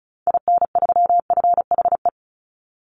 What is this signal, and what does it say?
Morse code, heard as a keyed tone, about 34 wpm, 700 Hz. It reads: IN3FHE
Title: Call Signs Repeated 2x - 35wpm